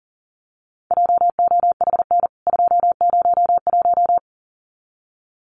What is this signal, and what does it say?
Morse code, heard as a keyed tone, about 40 wpm, 700 Hz. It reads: JOHN 201
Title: CWT Exchange 40wpm